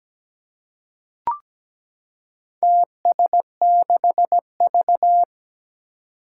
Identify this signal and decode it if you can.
Morse code, heard as a keyed tone, about 17 wpm, 700 Hz. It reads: TS6V
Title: Four Challenging Letter-Number Combinations 17wpm